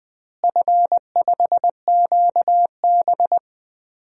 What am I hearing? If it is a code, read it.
Morse code, heard as a keyed tone, about 20 wpm, 700 Hz. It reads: F5QB
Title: DXCC-100 Call Signs - 4 Characters - Repeated 3x - 20wpm